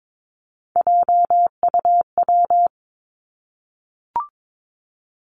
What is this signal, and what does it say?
Morse code, heard as a keyed tone, about 22 wpm, 700 Hz. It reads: JUW
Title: Three Challenging Letter-Number Combinations 22wpm